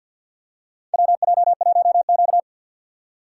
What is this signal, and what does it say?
Morse code, heard as a keyed tone, about 52 wpm, 700 Hz. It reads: WJ1Q
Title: USA Call Signs - 4x Character Spacing - 50wpm